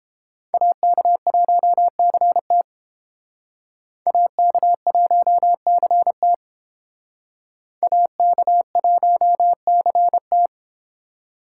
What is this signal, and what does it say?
Morse code, heard as a keyed tone, about 29 wpm, 700 Hz. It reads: AK1CT AK1CT AK1CT